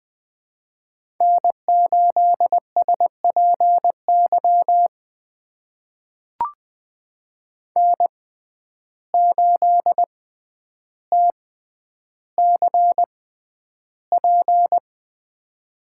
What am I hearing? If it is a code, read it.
Morse code, heard as a keyed tone, about 20 wpm, spaced as for 10 wpm, 700 Hz. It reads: N8SPY N8TCP
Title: USA Call Signs - 5 Characters - 6x Character Spacing - 20wpm